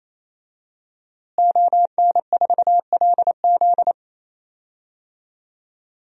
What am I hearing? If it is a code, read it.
Morse code, heard as a keyed tone, about 28 wpm, 700 Hz. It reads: ON4LZ
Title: DXCC-100 Call Signs - 5 Characters - Repeated 3x - 28wpm